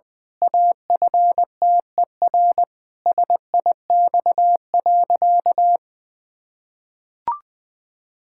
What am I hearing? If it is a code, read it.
Morse code, heard as a keyed tone, about 20 wpm, 700 Hz. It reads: AFTER SIX.
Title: Pangram Challenge v2 - 20wpm